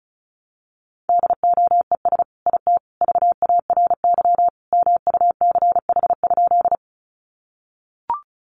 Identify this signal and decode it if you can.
Morse code, heard as a keyed tone, about 34 wpm, 700 Hz. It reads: DOES IT VARY MUCH?
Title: Sentences from Top 1200 Words 35wpm